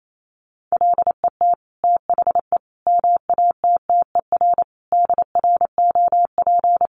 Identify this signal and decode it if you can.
Morse code, heard as a keyed tone, about 28 wpm, 700 Hz. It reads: LET THE MATTER DROP
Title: Sentences from Top 1400 Words 28wpm